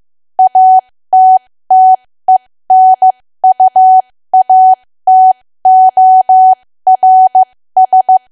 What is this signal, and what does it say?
Morse code, heard as a keyed tone, about 15 wpm, spaced as for 13 wpm, 750 Hz. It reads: ATTENUATORS
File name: cwa / W1AW-Level2 / 13WPM